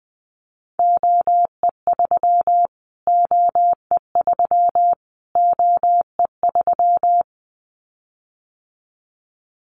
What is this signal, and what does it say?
Morse code, heard as a keyed tone, about 20 wpm, 700 Hz. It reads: OE3 OE3 OE3
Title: Callsign Prefixes for Top 25 DXCC - Repeated 3x - 20wpm